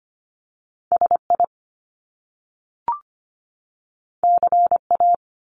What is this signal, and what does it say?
Morse code, heard as a keyed tone, about 25 wpm, 700 Hz. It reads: SI CA
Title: Top Two Letter Combinations in Words 25wpm